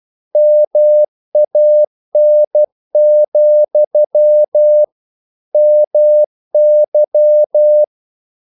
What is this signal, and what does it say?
Morse code, heard as a keyed tone, about 12 wpm, 600 Hz. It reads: MAN, MY
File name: CW / Practice / If12